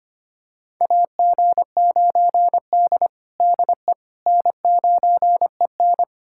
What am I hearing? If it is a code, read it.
Morse code, heard as a keyed tone, about 25 wpm, 700 Hz. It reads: AG9D DE N9EN
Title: USA Callsign DE Callsign - 4 Characters - 25wpm